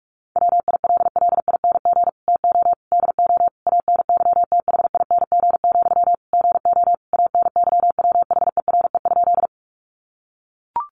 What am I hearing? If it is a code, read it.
Morse code, heard as a keyed tone, about 44 wpm, 700 Hz. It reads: WILLING TO DO ANYTHING, GO ANYWHERE?
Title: Sentences from Top 1400 Words 45wpm